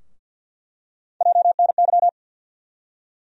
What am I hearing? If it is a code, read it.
Morse code, heard as a keyed tone, about 47 wpm, 700 Hz. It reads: JNY
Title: USA Call Sign Postfixes - 3 Letters 50wpm